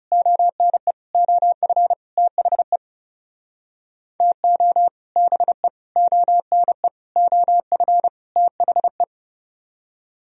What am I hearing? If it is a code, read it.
Morse code, heard as a keyed tone, about 40 wpm, 700 Hz. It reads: ONE OF THE TO BE ONE OF THE